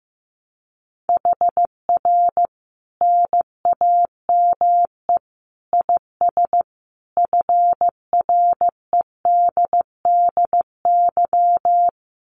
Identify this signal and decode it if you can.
Morse code, heard as a keyed tone, about 15 wpm, 700 Hz. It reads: HR NAME IS FREDDY
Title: Common QSO Phrases 15wpm